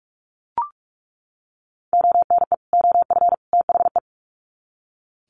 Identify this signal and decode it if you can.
Morse code, heard as a keyed tone, about 44 wpm, 700 Hz. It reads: ONE OF THE
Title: Sets of 3 Words 45wpm